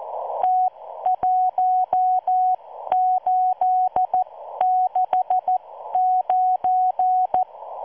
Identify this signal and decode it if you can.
Morse code, heard as a keyed tone, about 14 wpm, spaced as for 12 wpm, 740 Hz. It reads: T1869